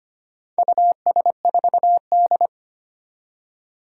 Morse code, heard as a keyed tone, about 25 wpm, 700 Hz. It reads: US4D
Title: Four Challenging Letter-Number Combinations 25wpm